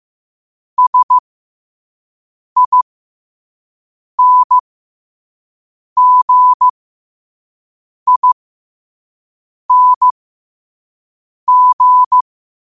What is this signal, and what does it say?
Morse code, heard as a keyed tone, about 15 wpm, spaced as for 5 wpm, 1000 Hz. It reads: SINGING
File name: qotdfiles / 5wpm